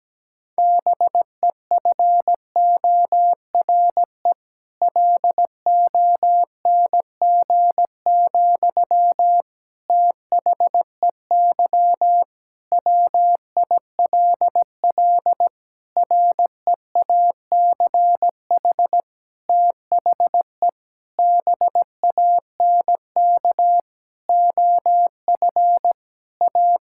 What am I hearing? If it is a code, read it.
Morse code, heard as a keyed tone, about 17 wpm, 700 Hz. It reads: BEFORE LONG, THEY WILL REACH THE BANK OF A
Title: Sentences from Top 1000 Words - Review Part 6 - 17wpm